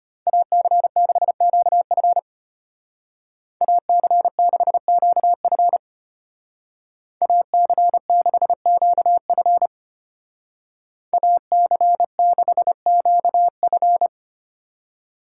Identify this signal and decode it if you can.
Morse code, heard as a keyed tone, about 31 wpm, 700 Hz. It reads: AC6QF AC6QF AC6QF AC6QF
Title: Speed Racing - USA 2x2 Call Signs 25wpm